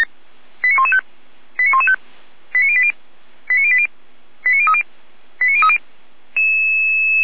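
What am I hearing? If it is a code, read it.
Five-tone selective call (ZVEI), heard as quick runs of five tones, tones of about 70 ms. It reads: 89115 89115 89080 89080 89030 89030
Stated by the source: ZVEI five-tone selective call received on SDR.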